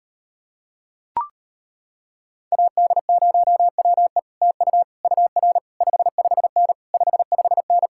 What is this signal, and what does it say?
Morse code, heard as a keyed tone, about 33 wpm, 700 Hz. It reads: AD0WE TU UR 55N 55N